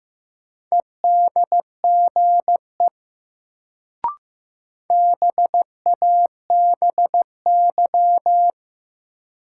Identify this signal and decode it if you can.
Morse code, heard as a keyed tone, about 16 wpm, 700 Hz. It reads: EDGE BABY
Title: Sentences from Top 1300 Words - New Word Drill - 15wpm